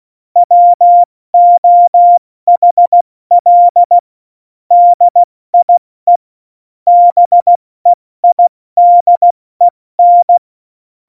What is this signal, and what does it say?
Morse code, heard as a keyed tone, about 16 wpm, 700 Hz. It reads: WOHL DIE BEIDEN
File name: Morsekurs / Witze / Witz 29. Oktober 2019, 70 F80